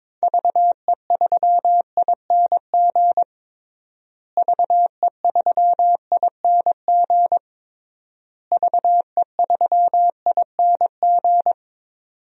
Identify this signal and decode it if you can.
Morse code, heard as a keyed tone, about 22 wpm, 700 Hz. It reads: VE3ING VE3ING VE3ING
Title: DXCC-100 Call Signs - 6 Characters - Repeated 3x - 22wpm